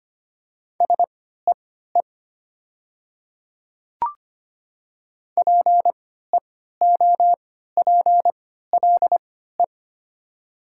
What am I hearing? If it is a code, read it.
Morse code, heard as a keyed tone, about 25 wpm, spaced as for 14 wpm, 700 Hz. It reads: SEE PEOPLE
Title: 3x Character Spacing - Top 100 Words - 25wpm